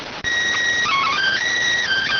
SSTV (slow-scan television), PD 290.